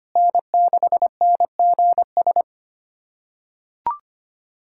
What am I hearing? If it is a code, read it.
Morse code, heard as a keyed tone, about 25 wpm, 700 Hz. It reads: N6NGS
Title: SOTA QSO Practice 25wpm